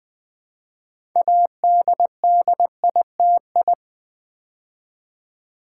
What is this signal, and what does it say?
Morse code, heard as a keyed tone, about 20 wpm, 700 Hz. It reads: ADDITI
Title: Build A Word - 2 Letters At A Time - 8-Letter Words - 20wpm